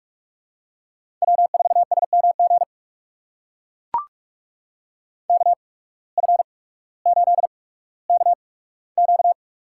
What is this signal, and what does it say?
Morse code, heard as a keyed tone, about 44 wpm, spaced as for 19 wpm, 700 Hz. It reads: W4SMG KF8KQ
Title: USA Call Signs - 5 Characters - 8x Character Spacing - 45wpm